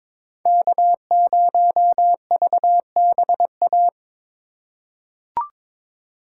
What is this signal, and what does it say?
Morse code, heard as a keyed tone, about 22 wpm, 700 Hz. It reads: K0VBA
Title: USA Call Signs - 5 Characters - 6x Character Spacing - 22wpm